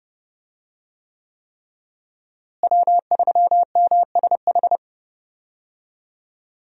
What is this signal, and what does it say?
Morse code, heard as a keyed tone, about 30 wpm, 700 Hz. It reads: W3MSH
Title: USA Call Signs - 5 Characters - Repeated 3x - 30wpm